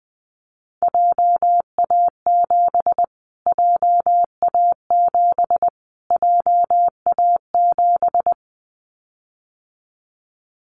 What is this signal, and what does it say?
Morse code, heard as a keyed tone, about 20 wpm, 700 Hz. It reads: JA7 JA7 JA7
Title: Callsign Prefixes for Top 25 DXCC - Repeated 3x - 20wpm